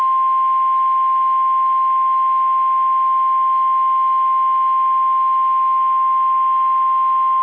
Emergency Alert System (SAME) tones (the 1050 Hz attention tone). It is heard from 1.1 to 6.7 s.
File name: eas / KSDZKDJL / KSDZKDJL